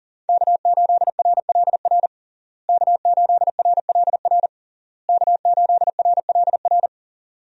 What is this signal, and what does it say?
Morse code, heard as a keyed tone, about 40 wpm, 700 Hz. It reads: K8RLR K8RLR K8RLR
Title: USA Call Signs - 5 Characters - Repeated 3x - 40wpm